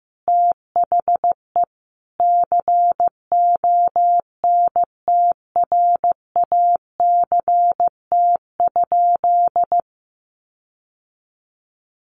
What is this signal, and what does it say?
Morse code, heard as a keyed tone, about 15 wpm, 700 Hz. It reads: THE CONTRACT?
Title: Sentences from Top 1400 Words 15wpm